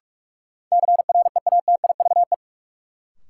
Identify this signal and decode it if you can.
Morse code, heard as a keyed tone, about 46 wpm, 700 Hz. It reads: CREATIVE
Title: Sentences from Top 1000 Words - Review Part 5 - New Word Drill - 45wpm